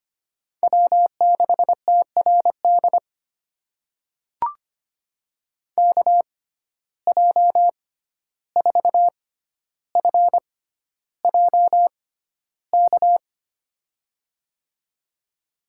Morse code, heard as a keyed tone, about 25 wpm, spaced as for 12 wpm, 700 Hz. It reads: W6TRD KJ4FJK
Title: USA Call Signs - 6x Character Spacing - 25wpm